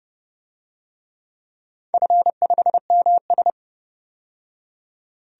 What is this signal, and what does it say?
Morse code, heard as a keyed tone, about 30 wpm, 700 Hz. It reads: F5MS
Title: DXCC-100 Call Signs - 4 Characters - 30wpm